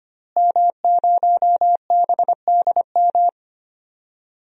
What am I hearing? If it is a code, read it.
Morse code, heard as a keyed tone, about 25 wpm, 700 Hz. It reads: M0BDM
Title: DXCC-100 Call Signs - 5 Characters - Repeated 3x - 25wpm